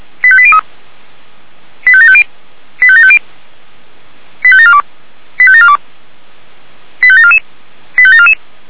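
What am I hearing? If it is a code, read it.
Five-tone selective call (ZVEI), heard as quick runs of five tones, tones of about 70 ms. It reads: 85803 85750 85750 85742 85742 85744 85744
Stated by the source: ZVEI five-tone call received by SDR, hosted by Feuerwehr Orlamünde (f85360).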